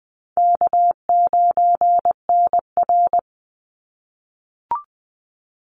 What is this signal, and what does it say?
Morse code, heard as a keyed tone, about 20 wpm, 700 Hz. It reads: K9NR
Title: Easy Call Signs - 20wpm